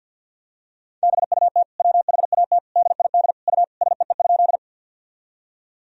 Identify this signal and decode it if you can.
Morse code, heard as a keyed tone, about 48 wpm, 700 Hz. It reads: BUT WHAT DID U SEE?
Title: Sentences from Top 100 Words and Abbreviations 50wpm